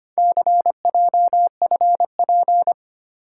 Morse code, heard as a keyed tone, about 25 wpm, 700 Hz. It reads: CJFP
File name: practice / 25wpm / Four Letter-Number Combinations - 25wpm